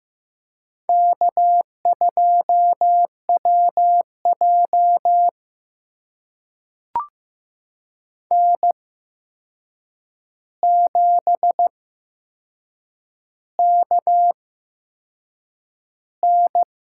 Morse code, heard as a keyed tone, about 15 wpm, spaced as for 6 wpm, 700 Hz. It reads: K2WJ N7KN
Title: USA Call Signs - 4 Characters - 8x Character Spacing - 15wpm